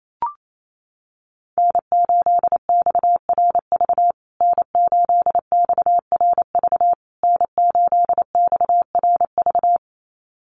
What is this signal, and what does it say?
Morse code, heard as a keyed tone, about 28 wpm, 700 Hz. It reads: N8XRV N8XRV N8XRV
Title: USA Call Signs Repeated 3x 28wpm